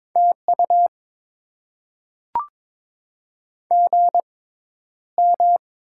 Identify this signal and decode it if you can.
Morse code, heard as a keyed tone, about 21 wpm, spaced as for 11 wpm, 700 Hz. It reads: TU GM
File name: Top 20 QSO Elements - 6x Character Spacing - 22wpm